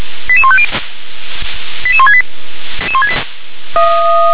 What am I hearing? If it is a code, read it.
Five-tone selective call (ZVEI), heard as quick runs of five tones, tones of about 70 ms. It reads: 88150 88168
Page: ZVEI